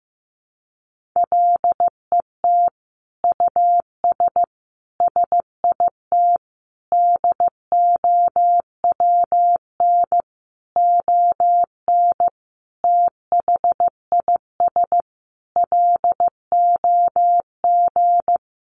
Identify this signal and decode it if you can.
Morse code, heard as a keyed tone, about 15 wpm, 700 Hz. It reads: LET US SIT DOWN ON THIS LOG
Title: Sentences from Top 1400 Words 15wpm